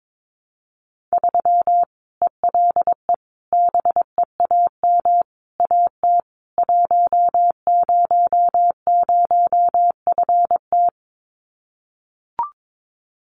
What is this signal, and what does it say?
Morse code, heard as a keyed tone, about 22 wpm, 700 Hz. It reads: 3 ELE BEAM AT 100FT
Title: Common QSO Phrases 22wpm